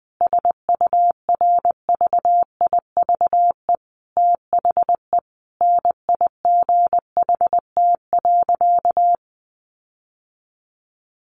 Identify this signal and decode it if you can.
Morse code, heard as a keyed tone, about 19 wpm, 700 Hz. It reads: SURVIVE THE NIGHT.
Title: Sentences from Top 1300 Words 20wpm